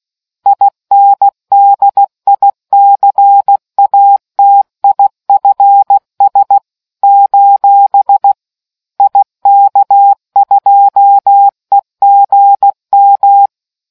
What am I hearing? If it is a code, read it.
Morse code, heard as a keyed tone, about 16 wpm, 800 Hz. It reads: INDICATIFS: IK2EGM